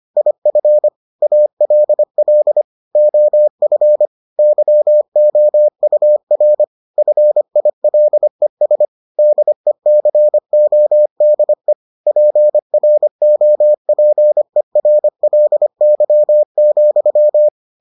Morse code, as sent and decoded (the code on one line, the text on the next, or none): .. ..-. / .- .-.. .-.. / --- ..-. / -.-- --- ..- .-. / ..-. .. .-.. ... / -.. -.-. --- -.. / .--. .-. --- .--. .-. .-.. -.-- --..--
IF ALL OF YOUR FILES DECODE PROPERLY,